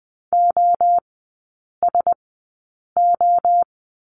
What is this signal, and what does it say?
Morse code, heard as a keyed tone, about 20 wpm, spaced as for 8 wpm, 700 Hz. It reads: OSO
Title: 041 Lesson 8e - Number 1 - Three Characters 20wpm